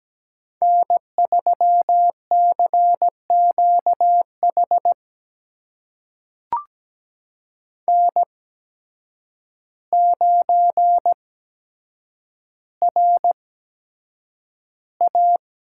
Morse code, heard as a keyed tone, about 17 wpm, spaced as for 8 wpm, 700 Hz. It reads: N3CQH N9RA
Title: USA Call Signs - 8x Character Spacing - 17wpm